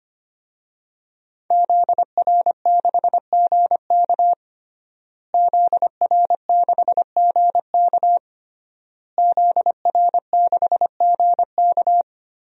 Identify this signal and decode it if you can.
Morse code, heard as a keyed tone, about 25 wpm, 700 Hz. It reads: ZR6GK ZR6GK ZR6GK
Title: DXCC-100 Call Signs - 5 Characters - Repeated 3x - 25wpm